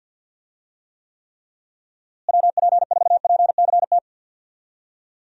Morse code, heard as a keyed tone, about 50 wpm, 700 Hz. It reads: WP4PCT
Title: USA Call Signs - 6 Characters - Repeated 3x - 50wpm